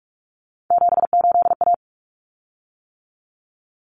Morse code, heard as a keyed tone, about 44 wpm, 700 Hz. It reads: 78A
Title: Common Cut Numbers - 3-Digit 45wpm